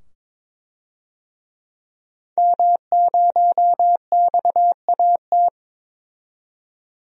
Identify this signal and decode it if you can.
Morse code, heard as a keyed tone, about 22 wpm, 700 Hz. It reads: M0XAT
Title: DXCC-100 Call Signs - 5 Characters - 22wpm